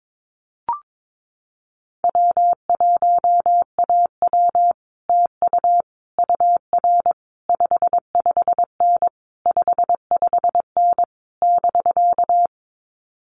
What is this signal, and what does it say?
Morse code, heard as a keyed tone, about 22 wpm, 700 Hz. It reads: W1AW TU UR 55N 55N <BK>
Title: Common QSO Phrases Encore 22wpm